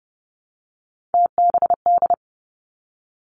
Morse code, heard as a keyed tone, about 30 wpm, 700 Hz. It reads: TBD
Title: Three Challenging Letter-Number Combinations 30wpm